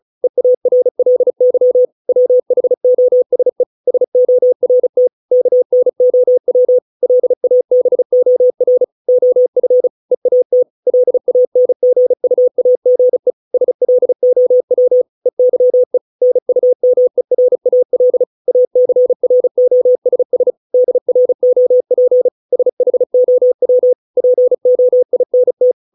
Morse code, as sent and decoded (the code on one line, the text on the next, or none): . .- .-. .-.. -.-- / .-- .... --- ... / ... --- .-. - / -.- -. --- .-- / .-.. .- -... --- .-. / --- ..-. / . .- - / .-.. .- -. --. ..- .- --. / ... .-.. --- .-- / . -.-- . / -. ..- -- . .-. .- .-.. / .- -.-. .-. --- ... ... / -.. .-. --- .--. / ... .... --- .-- / .--. --- .. -. -
EARLY WHOSE SORT KNOW LABOR OF EAT LANGUAGE SLOW EYE NUMERAL ACROSS DROP SHOW POINT